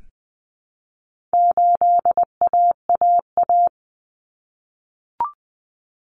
Morse code, heard as a keyed tone, about 20 wpm, 700 Hz. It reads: 8AAA
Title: Common Cut Numbers - 4-Digit 20wpm